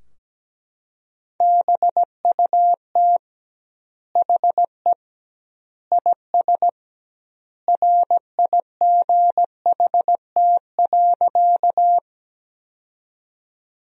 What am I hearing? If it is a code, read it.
Morse code, heard as a keyed tone, about 17 wpm, 700 Hz. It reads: BUT HE IS RIGHT.
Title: 2x Word-Spacing - Sentences from Top 200 Words - 17wpm